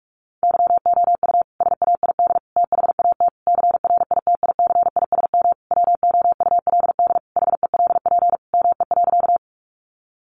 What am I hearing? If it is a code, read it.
Morse code, heard as a keyed tone, about 44 wpm, 700 Hz. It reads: YOU SAID THAT CRITICISM WOULD HELP ME.
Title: Sentences from Top 1300 Words 45wpm